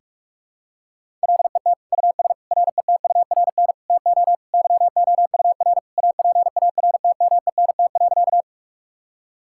Morse code, heard as a keyed tone, about 46 wpm, 700 Hz. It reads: LET US RETURN TO YOUR APARTMENT.
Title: Sentences from Top 1000 Words - Review Part 6 - 45wpm